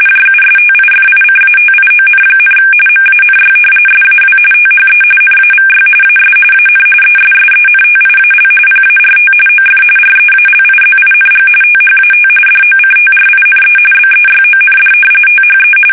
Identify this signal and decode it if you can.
RTTY, about 45.45 baud. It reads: JUSTIFICATION, MANY SAW THE NAVY AS AN ESSENTIAL PART OF REALIZING THAT DOCTRINE BEYOND THE
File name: fsk850 / usnavy3